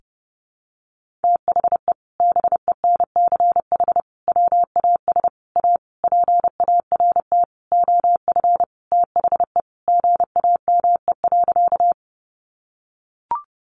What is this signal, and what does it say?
Morse code, heard as a keyed tone, about 30 wpm, 700 Hz. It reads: THE BENCH WAS A PART OF THE GAME.
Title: Sentences from Top 1400 Words 30wpm